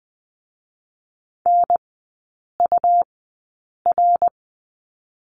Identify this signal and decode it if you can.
Morse code, heard as a keyed tone, about 20 wpm, spaced as for 8 wpm, 700 Hz. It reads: NUR
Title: 184 Lesson 30e - Letter b - Three Characters 20wpm